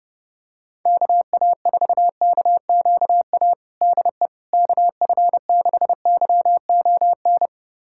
Morse code, heard as a keyed tone, about 32 wpm, 700 Hz. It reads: KA4KQA DE KF6YON
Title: USA Callsign DE Callsign 30wpm